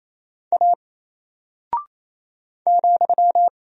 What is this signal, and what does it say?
Morse code, heard as a keyed tone, about 28 wpm, 700 Hz. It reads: A ,
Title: Single Character (Letter:Number:Punctuation:Prosign) 28wpm